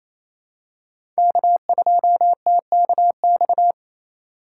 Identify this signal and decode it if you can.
Morse code, heard as a keyed tone, about 28 wpm, 700 Hz. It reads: K2TKX
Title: USA Call Signs - 5 Characters - 4x Character Spacing - 28wpm